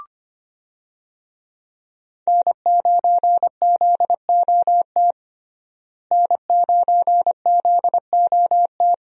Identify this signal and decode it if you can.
Morse code, heard as a keyed tone, about 25 wpm, 700 Hz. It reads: N9ZOT N9ZOT
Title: USA Call Signs - 5 Characters - Repeated 2x - 25wpm